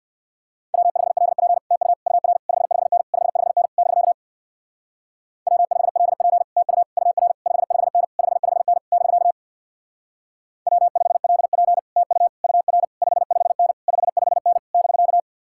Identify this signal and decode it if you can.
Morse code, heard as a keyed tone, about 58 wpm, 700 Hz. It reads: W5LP TU UR 55N 55N <BK> W5LP TU UR 55N 55N <BK> W5LP TU UR 55N 55N <BK>
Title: Speed-Racing Common QSO Phrases - 50wpm